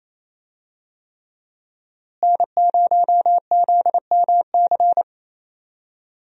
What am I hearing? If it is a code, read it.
Morse code, heard as a keyed tone, about 28 wpm, 700 Hz. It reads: N0ZMC
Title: USA Call Signs - 5 Characters - Repeated 3x - 28wpm